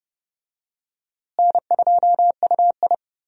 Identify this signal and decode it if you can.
Morse code, heard as a keyed tone, about 30 wpm, 700 Hz. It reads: N2UI
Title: USA Call Signs - 4 Characters - Repeated 3x - 30wpm